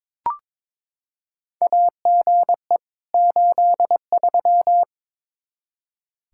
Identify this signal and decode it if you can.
Morse code, heard as a keyed tone, about 22 wpm, 700 Hz. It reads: AGE 83
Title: Practice QSO - Exchange 3 - More Info - 22wpm